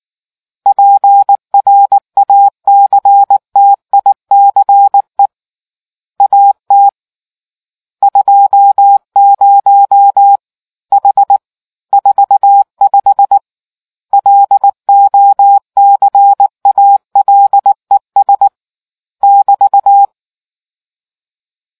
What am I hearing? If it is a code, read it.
Morse code, heard as a keyed tone, about 19 wpm, 800 Hz. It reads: PRACTICE AT 20 H 45 LOCALES <BT>